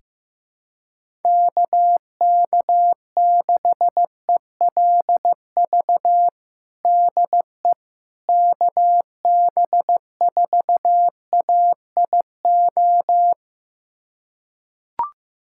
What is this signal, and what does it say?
Morse code, heard as a keyed tone, about 15 wpm, 700 Hz. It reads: KK6ELV DE KB4AIO
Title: USA Callsign DE Callsign - 6 Characters - 15wpm